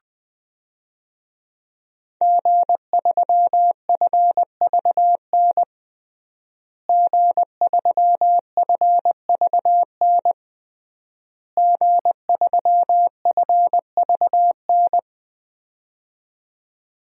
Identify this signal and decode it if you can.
Morse code, heard as a keyed tone, about 20 wpm, 700 Hz. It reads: G3FVN G3FVN G3FVN
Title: DXCC-100 Call Signs - 5 Characters - Repeated 3x - 20wpm